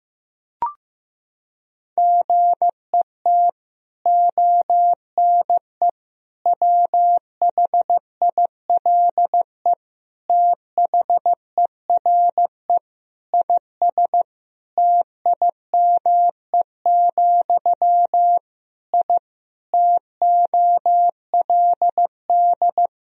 Morse code, heard as a keyed tone, about 16 wpm, 700 Hz. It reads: GET ONE WHILE THERE IS TIME, I TOLD
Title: Sentences from Top 1000 Words - Review Part 1 - 15wpm